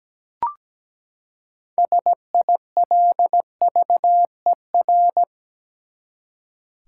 Morse code, heard as a keyed tone, about 17 wpm, 700 Hz. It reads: SILVER
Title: Six Letter Words - 17wpm